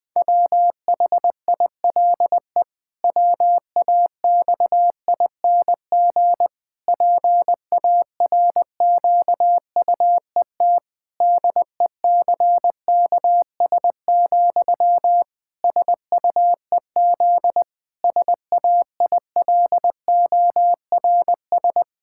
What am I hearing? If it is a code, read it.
Morse code, heard as a keyed tone, about 21 wpm, 700 Hz. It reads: WHILE WAXING PARQUET DECKS, SUEZ SAILORS
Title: Pangram Challenge v2 - 20wpm